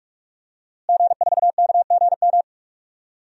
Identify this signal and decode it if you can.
Morse code, heard as a keyed tone, about 43 wpm, 700 Hz. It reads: G4KGM
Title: DXCC-100 Call Signs - 5 Characters - Repeated 2x - 45wpm